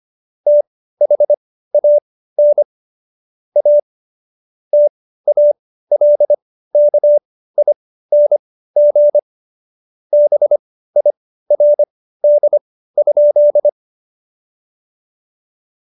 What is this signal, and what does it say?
Morse code, heard as a keyed tone, about 25 wpm, spaced as for 15 wpm, 590 Hz. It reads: THAN A TALKING BIRD?
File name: Jokes-25-char-15-eff